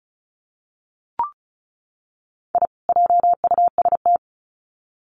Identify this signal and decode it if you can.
Morse code, heard as a keyed tone, about 34 wpm, 700 Hz. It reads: I JUST
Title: Sets of 2 Words 35wpm